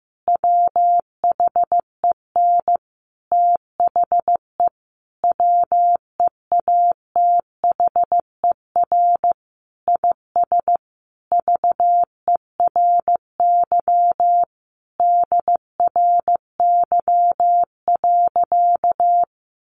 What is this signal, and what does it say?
Morse code, heard as a keyed tone, about 15 wpm, 700 Hz. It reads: WHEN THE WEATHER IS VERY DRY.
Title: Sentences from Top 1400 Words 15wpm